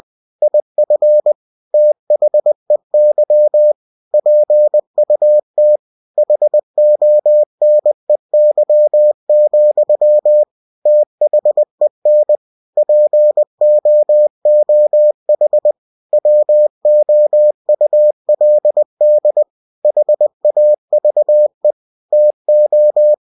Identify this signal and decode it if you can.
Morse code, heard as a keyed tone, about 20 wpm, 600 Hz. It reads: IF THEY PUT HONEY, THEN POOH WOULD HAVE TO
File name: cw-books / winnie-the-pooh / winnie-20wpm-page-0037